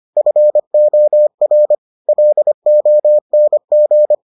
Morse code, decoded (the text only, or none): FOR LONG